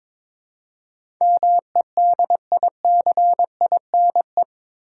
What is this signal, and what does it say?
Morse code, heard as a keyed tone, about 21 wpm, 700 Hz. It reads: MEDICINE
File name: practice / 22wpm / Eight Letter Words - 22wpm